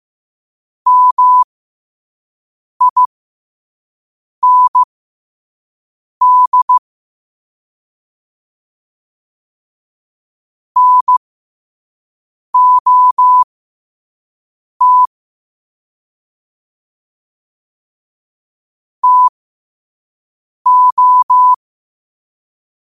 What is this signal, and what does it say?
Morse code, heard as a keyed tone, about 15 wpm, spaced as for 5 wpm, 1000 Hz. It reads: MIND NOT TO
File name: qotdfiles / 5wpm